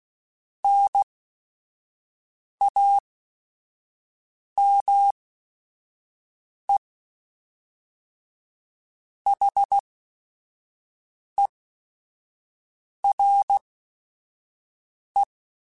Morse code, heard as a keyed tone, about 16 wpm, spaced as for 5 wpm, 800 Hz. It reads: NAME HERE